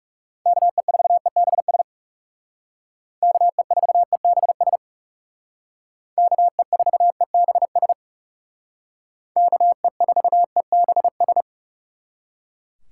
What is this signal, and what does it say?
Morse code, heard as a keyed tone, about 36 wpm, 700 Hz. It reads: KE4EBS KE4EBS KE4EBS KE4EBS
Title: Speed Racing - USA Call Signs 30wpm